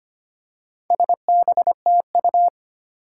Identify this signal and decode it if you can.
Morse code, heard as a keyed tone, about 25 wpm, 700 Hz. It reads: SBTU
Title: Four Challenging Letter-Number Combinations 25wpm